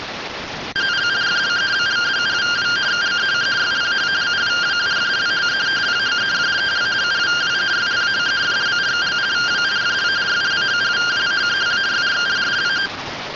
Two-tone teleprinter signal, about 170 Hz shift.